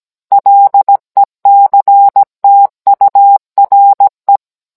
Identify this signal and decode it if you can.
Morse code, heard as a keyed tone, about 17 wpm, 800 Hz. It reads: LECTURE